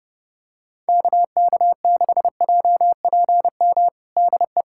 Morse code, heard as a keyed tone, about 32 wpm, 700 Hz. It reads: KK6JPM DE
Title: USA Callsign DE Callsign - 6 Characters - 30wpm